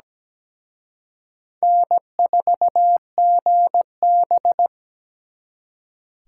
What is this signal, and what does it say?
Morse code, heard as a keyed tone, about 20 wpm, 700 Hz. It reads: N4GB
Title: Speed Racing - USA 1x2 Call Signs 17wpm